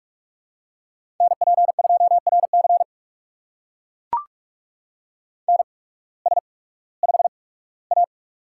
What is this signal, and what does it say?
Morse code, heard as a keyed tone, about 44 wpm, spaced as for 21 wpm, 700 Hz. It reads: NP2RC NS5A
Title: USA Call Signs - 8x Character Spacing - 45wpm